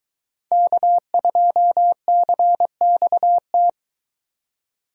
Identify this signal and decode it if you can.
Morse code, heard as a keyed tone, about 21 wpm, 700 Hz. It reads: K2CXT